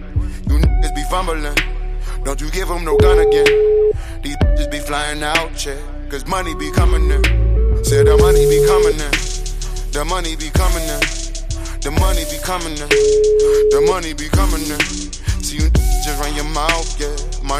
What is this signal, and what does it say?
Ringback tone (Europe, 425 Hz).